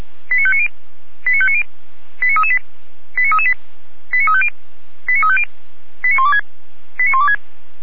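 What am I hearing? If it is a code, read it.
Five-tone selective call (ZVEI), heard as quick runs of five tones, tones of about 70 ms. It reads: 89590 89590 89338 89338 89350 89350 89126 89126
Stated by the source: ZVEI five-tone selective call received with an SDR.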